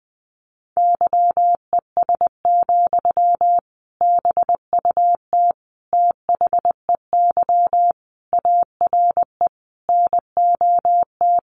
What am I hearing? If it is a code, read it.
Morse code, heard as a keyed tone, about 20 wpm, 700 Hz. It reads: YES, BUT THEY ARE NOT
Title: Sentences from Top 1400 Words 20wpm